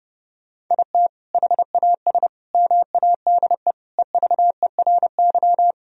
Morse code, heard as a keyed tone, about 30 wpm, 700 Hz. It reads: IT HAS MADE EVERY
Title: Sentences from Top 1000 Words - Review Part 3 - 30wpm